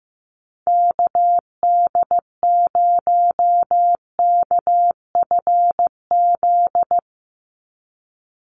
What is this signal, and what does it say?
Morse code, heard as a keyed tone, about 15 wpm, 700 Hz. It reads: KD0KFZ
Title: USA Call Signs 15wpm